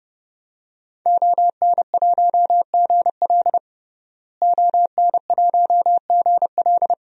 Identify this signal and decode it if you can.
Morse code, heard as a keyed tone, about 30 wpm, 700 Hz. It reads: ON1GL ON1GL
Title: DXCC-100 Call Signs - 5 Characters - Repeated 2x - 30wpm